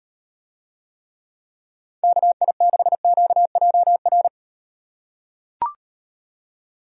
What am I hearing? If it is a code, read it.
Morse code, heard as a keyed tone, about 38 wpm, 700 Hz. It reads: KI6QJR